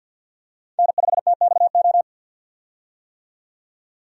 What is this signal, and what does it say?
Morse code, heard as a keyed tone, about 52 wpm, 700 Hz. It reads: N5TXO
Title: USA Call Signs - 5 Characters - 50wpm